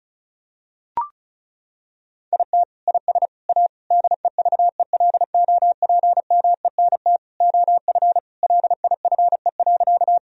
Morse code, heard as a keyed tone, about 36 wpm, 700 Hz. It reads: IT IS A DEVELOPMENT OF LIFE.
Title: Sentences from Top 1000 Words - Review Part 4 - 35wpm